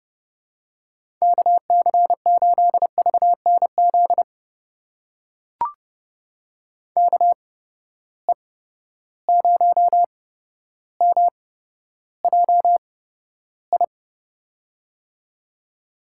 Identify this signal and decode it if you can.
Morse code, heard as a keyed tone, about 30 wpm, spaced as for 13 wpm, 700 Hz. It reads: KC8VNZ KE0MJI
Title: USA Call Signs - 6 Characters - 8x Character Spacing - 30wpm